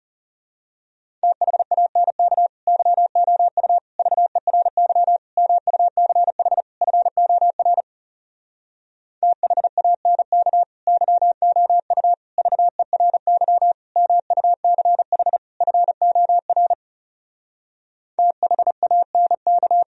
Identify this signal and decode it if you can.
Morse code, heard as a keyed tone, about 34 wpm, 700 Hz. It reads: THANK YOU VERY MUCH FOR THANK YOU VERY MUCH FOR THANK
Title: Speed-Racing - Sets of 5 Words 30wpm